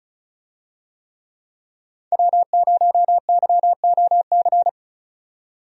Morse code, heard as a keyed tone, about 36 wpm, 700 Hz. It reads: W0YOC
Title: USA Call Signs - 5 Characters - 35wpm